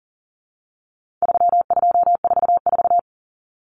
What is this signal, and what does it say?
Morse code, heard as a keyed tone, about 40 wpm, 700 Hz. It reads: 3244
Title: Common Cut Numbers - 4-Digit 40wpm